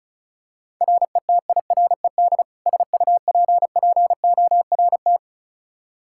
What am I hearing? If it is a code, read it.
Morse code, heard as a keyed tone, about 34 wpm, 700 Hz. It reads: RETIRED SUPPORT
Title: Practice QSO - Exchange 3 - More Info - 35wpm